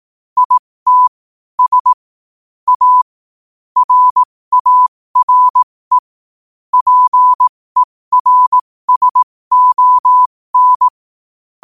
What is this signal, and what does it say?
Morse code, heard as a keyed tone, about 18 wpm, spaced as for 15 wpm, 1000 Hz. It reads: IT S A RARE PERSON
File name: qotdfiles / 15wpm